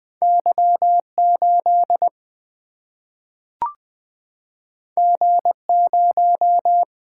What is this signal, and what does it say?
Morse code, heard as a keyed tone, about 20 wpm, 700 Hz. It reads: Y 8 G 0
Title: Single-Letter Single-Number 20wpm